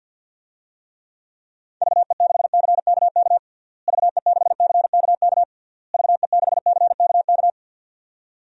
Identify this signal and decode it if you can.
Morse code, heard as a keyed tone, about 52 wpm, 700 Hz. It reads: VE6CKK VE6CKK VE6CKK
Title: DXCC-100 Call Signs - 6 Characters - Repeated 3x - 50wpm